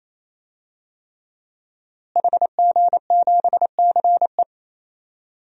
Morse code, heard as a keyed tone, about 28 wpm, 700 Hz. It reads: HG7CE
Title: DXCC-100 Call Signs - 5 Characters - Repeated 3x - 28wpm